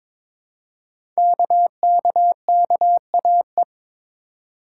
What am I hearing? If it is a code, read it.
Morse code, heard as a keyed tone, about 22 wpm, 700 Hz. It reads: KKKAE
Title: Code Groups - 5 Letters - 22wpm